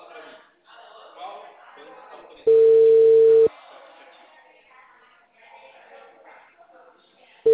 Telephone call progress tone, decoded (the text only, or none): ringback tone